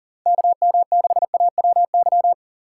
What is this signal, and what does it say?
Morse code, heard as a keyed tone, about 37 wpm, 700 Hz. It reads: KM6AWY
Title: USA Call Signs - 6 Characters - 6x Character Spacing - 40wpm